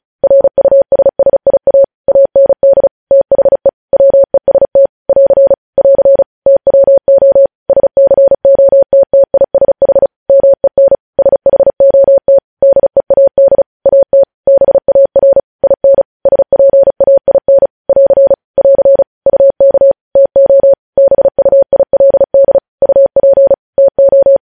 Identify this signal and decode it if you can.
Morse code, heard as a keyed tone, about 36 wpm, 550 Hz. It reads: RUSSIA AND THE WEST <AR> <AR> TWO SCOTTISH MEN SHOT DEAD AT BAR IN SPAIN <AR> <AR> UK TO BUILD UP TO
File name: podcast / News headlines-1x-35x35